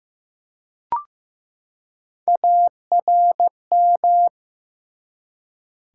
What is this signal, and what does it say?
Morse code, heard as a keyed tone, about 15 wpm, 700 Hz. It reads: ARM
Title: Three Letter Words - 15wpm